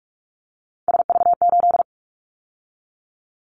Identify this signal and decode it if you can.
Morse code, heard as a keyed tone, about 46 wpm, 700 Hz. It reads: SV8
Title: Callsign Prefixes for Top 50 DXCC - 45wpm